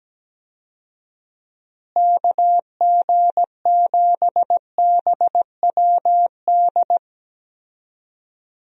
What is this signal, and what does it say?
Morse code, heard as a keyed tone, about 17 wpm, 700 Hz. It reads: KG7BWD
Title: USA Call Signs - 6 Characters - Repeated 3x - 17wpm